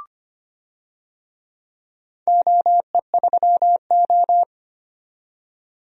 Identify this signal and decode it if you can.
Morse code, heard as a keyed tone, about 25 wpm, 700 Hz. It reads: OE3O
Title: DXCC-100 Call Signs - 4 Characters - 25wpm